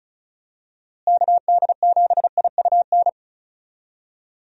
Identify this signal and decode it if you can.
Morse code, heard as a keyed tone, about 36 wpm, 700 Hz. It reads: KD7IUN
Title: USA Call Signs - 6 Characters - Repeated 3x - 35wpm